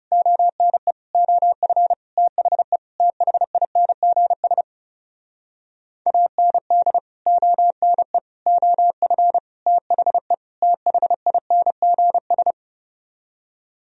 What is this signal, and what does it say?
Morse code, heard as a keyed tone, about 40 wpm, 700 Hz. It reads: ONE OF THE THINGS AND ONE OF THE THINGS